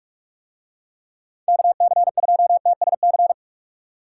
Morse code, heard as a keyed tone, about 46 wpm, 700 Hz. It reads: KC2TSC